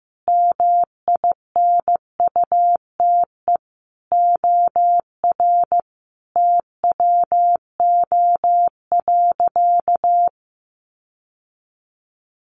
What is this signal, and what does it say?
Morse code, heard as a keyed tone, about 15 wpm, 700 Hz. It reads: MINUTE OR TWO.
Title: Sentences from Top 1400 Words 15wpm